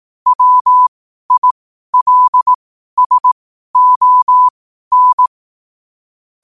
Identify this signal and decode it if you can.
Morse code, heard as a keyed tone, about 18 wpm, spaced as for 12 wpm, 1000 Hz. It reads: WILSON
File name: qotdfiles / 12wpm